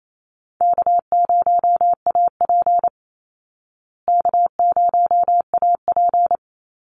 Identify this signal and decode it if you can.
Morse code, heard as a keyed tone, about 28 wpm, 700 Hz. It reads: K0AP K0AP
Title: Easy Call Signs Repeated 3x - 28wpm